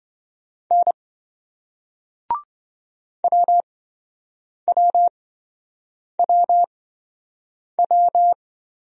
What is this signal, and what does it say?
Morse code, heard as a keyed tone, about 27 wpm, spaced as for 7 wpm, 700 Hz. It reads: N WWWW